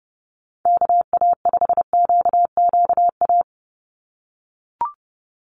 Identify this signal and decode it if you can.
Morse code, heard as a keyed tone, about 30 wpm, 700 Hz. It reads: KA5QQA
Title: NA SOTA Chaser Callsigns 30wpm